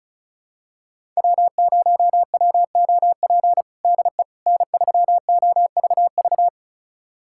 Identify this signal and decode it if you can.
Morse code, heard as a keyed tone, about 34 wpm, 700 Hz. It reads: W0WOP DE N3OVV
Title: USA Callsign DE Callsign - 5 Characters - 35wpm